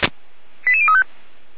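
Five-tone selective call (ZVEI), heard as quick runs of five tones, tones of about 70 ms. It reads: 88035
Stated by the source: ZVEI five-tone selective call received on an SDR.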